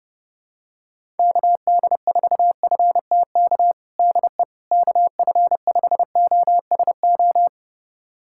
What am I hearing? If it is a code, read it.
Morse code, heard as a keyed tone, about 30 wpm, 700 Hz. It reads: KD4FTK DE KF5OSO
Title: USA Callsign DE Callsign 30wpm